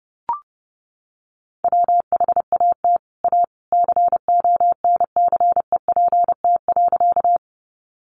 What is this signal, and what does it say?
Morse code, heard as a keyed tone, about 30 wpm, 700 Hz. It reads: WHAT A CONCEPT.
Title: Sentences from Top 1100 Words 30wpm